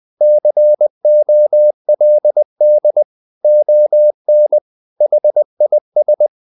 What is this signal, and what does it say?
Morse code, heard as a keyed tone, about 20 wpm, 600 Hz. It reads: COLD ON HIS
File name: the old man and the sea / the old man and the sea part 1 20wpm